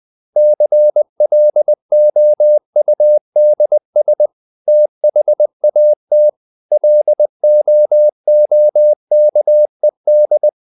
Morse code, heard as a keyed tone, about 20 wpm, 600 Hz. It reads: CLOUDS THAT LOOKED
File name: the old man and the sea / the old man and the sea part 1 20wpm